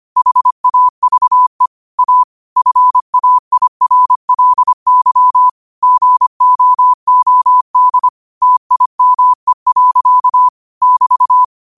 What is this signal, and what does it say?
Morse code, heard as a keyed tone, about 25 wpm, 1000 Hz. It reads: HAVE A FAIRLY GOOD TIME. <BT>
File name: qotdfiles / 25wpm